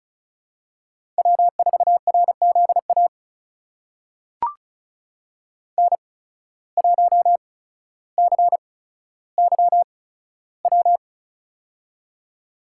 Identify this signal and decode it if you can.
Morse code, heard as a keyed tone, about 34 wpm, spaced as for 15 wpm, 700 Hz. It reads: W4RZA N1CYW
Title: USA Call Signs - 5 Characters - 8x Character Spacing - 35wpm